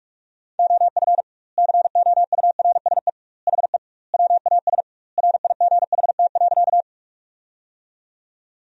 Morse code, heard as a keyed tone, about 41 wpm, 700 Hz. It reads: OF COURSE HE WAS RIGHT.
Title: 2x Word-Spacing - Sentences from Top 500 Words - 45wpm